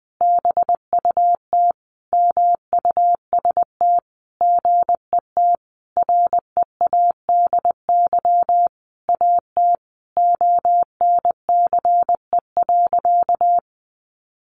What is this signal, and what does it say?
Morse code, heard as a keyed tone, about 19 wpm, 700 Hz. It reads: BUT MUST GET READY AT ONCE.
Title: Sentences from Top 1100 Words 20wpm